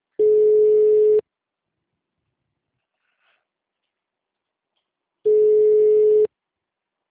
Ringback tone (Europe, 425 Hz).
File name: out- -2008- - - .2350